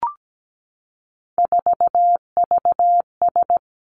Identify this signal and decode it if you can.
Morse code, heard as a keyed tone, about 17 wpm, 700 Hz. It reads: 4VS
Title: Three Challenging Letter-Number Combinations 17wpm